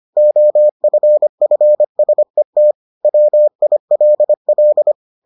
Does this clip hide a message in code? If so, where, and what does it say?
Yes: Morse code from 0.2 to 4.9 s, heard as a keyed tone, about 25 wpm, 600 Hz. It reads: OFFSET WILL